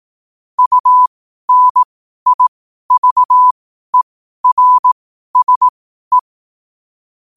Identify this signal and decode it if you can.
Morse code, heard as a keyed tone, about 18 wpm, spaced as for 12 wpm, 1000 Hz. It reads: UNIVERSE
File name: qotdfiles / 12wpm